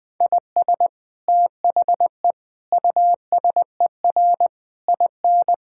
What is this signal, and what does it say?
Morse code, heard as a keyed tone, about 20 wpm, 700 Hz. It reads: IS THE USER IN
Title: Sentences from Top 1000 Words - Review Part 5 - 20wpm